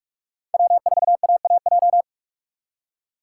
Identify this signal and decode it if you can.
Morse code, heard as a keyed tone, about 58 wpm, 700 Hz. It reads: W4AAJ